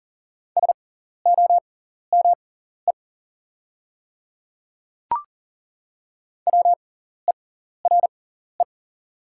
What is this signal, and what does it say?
Morse code, heard as a keyed tone, about 40 wpm, spaced as for 14 wpm, 700 Hz. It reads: SOME WERE
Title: 6x Character Spacing - Top 100 Words - 40wpm